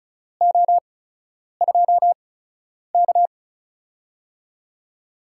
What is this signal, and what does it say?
Morse code, heard as a keyed tone, about 35 wpm, spaced as for 10 wpm, 700 Hz. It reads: O2K
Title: USA Call Signs - 4 Characters - 8x Character Spacing - 35wpm